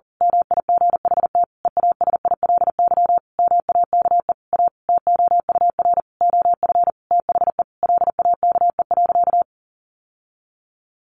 Morse code, heard as a keyed tone, about 40 wpm, 700 Hz. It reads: MIGHT EASILY MAKE A TOUR OF THE LAKE.
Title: Sentences from Top 1400 Words 40wpm